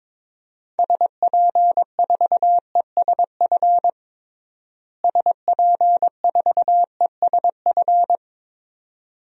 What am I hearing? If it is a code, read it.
Morse code, heard as a keyed tone, about 22 wpm, 700 Hz. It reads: SP4ESF SP4ESF
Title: DXCC-100 Call Signs - 6 Characters - Repeated 2x - 22wpm